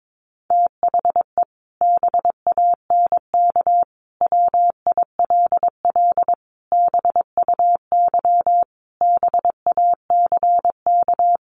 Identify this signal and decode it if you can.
Morse code, heard as a keyed tone, about 22 wpm, 700 Hz. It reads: THE BANK WILL BUY BACK
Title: Sentences from Top 1500 Words 22wpm